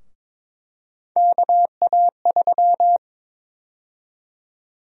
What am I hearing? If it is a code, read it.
Morse code, heard as a keyed tone, about 22 wpm, 700 Hz. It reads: KA3
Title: USA Call Sign Prefixes - 22wpm